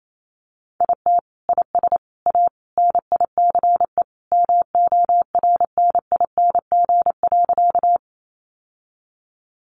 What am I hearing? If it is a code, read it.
Morse code, heard as a keyed tone, about 28 wpm, 700 Hz. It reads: IT IS A NICE MORNING.
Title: Sentences from Top 1100 Words 28wpm